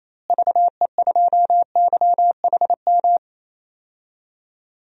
Morse code, heard as a keyed tone, about 28 wpm, 700 Hz. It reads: VE2YHM
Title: DXCC-100 Call Signs - 6 Characters - Repeated 2x - 28wpm